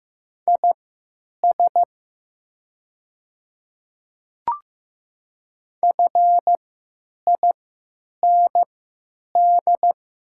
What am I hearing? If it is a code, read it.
Morse code, heard as a keyed tone, about 15 wpm, spaced as for 9 wpm, 700 Hz. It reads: IS FIND
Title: 3x Character Spacing - Top 100 Words - 15wpm